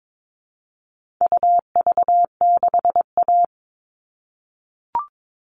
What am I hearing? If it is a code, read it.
Morse code, heard as a keyed tone, about 22 wpm, 700 Hz. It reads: UV6A
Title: All Cut Numbers - 4-Digit 22wpm